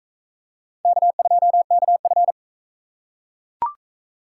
Morse code, heard as a keyed tone, about 36 wpm, 700 Hz. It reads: K2KF